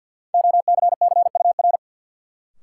Morse code, heard as a keyed tone, about 53 wpm, 700 Hz. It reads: OCCUR
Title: Sentences from Top 900 Words - New Word Drill - 50wpm